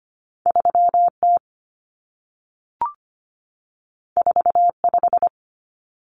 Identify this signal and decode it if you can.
Morse code, heard as a keyed tone, about 25 wpm, 700 Hz. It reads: 3T 45
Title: Common Cut Numbers - 2-Digit 25wpm